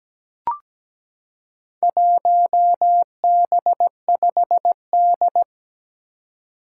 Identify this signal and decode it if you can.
Morse code, heard as a keyed tone, about 17 wpm, 700 Hz. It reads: 1B5D
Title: Four Challenging Letter-Number Combinations 17wpm